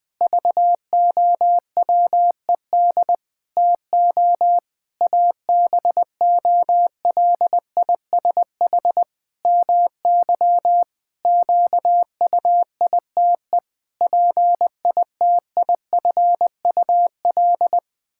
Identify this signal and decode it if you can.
Morse code, heard as a keyed tone, about 21 wpm, 700 Hz. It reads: VOWED TO ABOLISH MY QUITE PITIFUL
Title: Pangram Challenge v2 - 20wpm